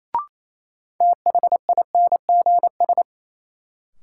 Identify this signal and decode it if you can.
Morse code, heard as a keyed tone, about 28 wpm, 700 Hz. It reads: THINGS
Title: Sentences from Top 1400 Words - New Word Drill Alt - 28wpm